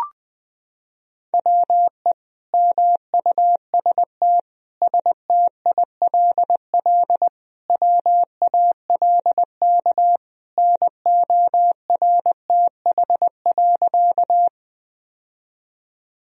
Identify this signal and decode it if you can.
Morse code, heard as a keyed tone, about 20 wpm, 700 Hz. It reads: WE MUST STILL WALK NORTH.
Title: Sentences from Top 1000 Words - Review Part 1 - 20wpm